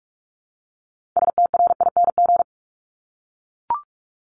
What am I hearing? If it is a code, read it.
Morse code, heard as a keyed tone, about 46 wpm, 700 Hz. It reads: STRING
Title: Multi-letter Phonemes 45wpm